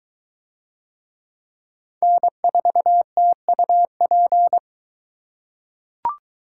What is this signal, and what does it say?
Morse code, heard as a keyed tone, about 23 wpm, 700 Hz. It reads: N4TUP